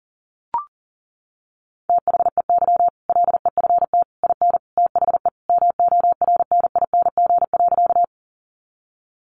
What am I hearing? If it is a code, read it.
Morse code, heard as a keyed tone, about 40 wpm, 700 Hz. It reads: THEY LEFT IN THE MORNING.
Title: Sentences from Top 400 Words 40wpm